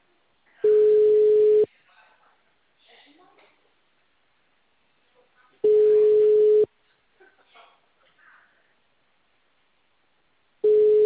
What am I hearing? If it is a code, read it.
Ringback tone (Europe, 425 Hz), with speech.
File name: out- -2055- - - .27770